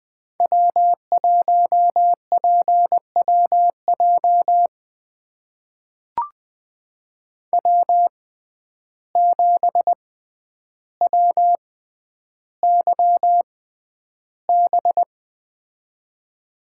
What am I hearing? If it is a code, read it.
Morse code, heard as a keyed tone, about 21 wpm, spaced as for 10 wpm, 700 Hz. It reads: W1PWJ W7WYB
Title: USA Call Signs - 5 Characters - 6x Character Spacing - 20wpm